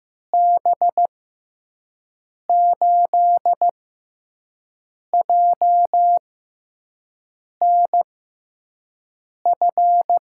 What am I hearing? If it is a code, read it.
Morse code, heard as a keyed tone, about 15 wpm, spaced as for 5 wpm, 700 Hz. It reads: B8JNF
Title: USA Call Signs - 6 Characters - 6x Character Spacing - 15wpm